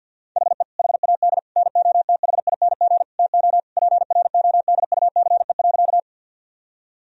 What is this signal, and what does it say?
Morse code, heard as a keyed tone, about 50 wpm, 700 Hz. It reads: HE HAD NOTHING TO PRODUCE.
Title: Sentences from Top 1000 Words - Review Part 3 - 50wpm